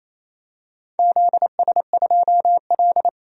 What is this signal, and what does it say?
Morse code, heard as a keyed tone, about 28 wpm, 700 Hz. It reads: ZS2L
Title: DXCC-100 Call Signs - 4 Characters - 28wpm